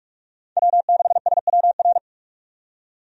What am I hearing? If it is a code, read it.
Morse code, heard as a keyed tone, about 43 wpm, 700 Hz. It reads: W6SWR
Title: USA Call Signs - 5 Characters - 45wpm